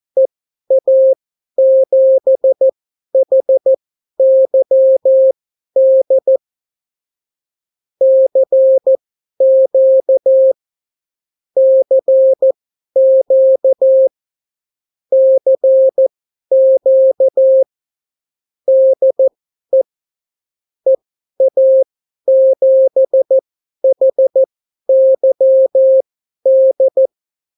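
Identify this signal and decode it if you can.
Morse code, heard as a keyed tone, about 14 wpm, spaced as for 11 wpm, 530 Hz. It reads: EA7HYD CQ CQ CQ DE EA7HYD
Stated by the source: Example Spanish CW QSO from a Morse learning book, 14 wpm characters with 8 wpm spacing.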